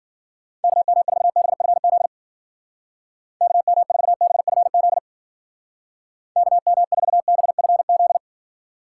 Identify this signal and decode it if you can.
Morse code, heard as a keyed tone, about 49 wpm, 700 Hz. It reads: KM4BFZ KM4BFZ KM4BFZ
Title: Speed Racing - USA Call Signs 40wpm